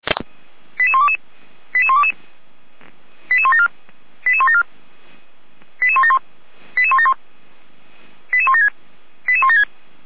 Five-tone selective call (ZVEI), heard as quick runs of five tones, tones of about 70 ms. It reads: 88122 88122 88174 88174 88171 88171 88176 88176
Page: SDR/ZVEI